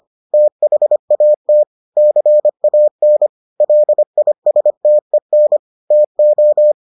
Morse code, heard as a keyed tone, about 25 wpm, 600 Hz. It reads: THAT CAN LISTEN TO